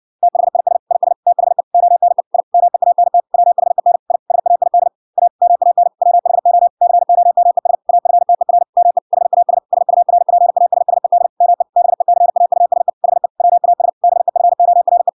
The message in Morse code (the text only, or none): THIS IS THE ONE I MEANT WHEN I SAID A MAN WHO COMES AFTER ME HAS SURPASSED ME BECAUSE HE WAS BEFORE